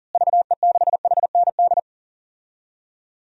Morse code, heard as a keyed tone, about 40 wpm, 700 Hz. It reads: VE6HND
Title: DXCC-100 Call Signs - 6 Characters - Repeated 2x - 40wpm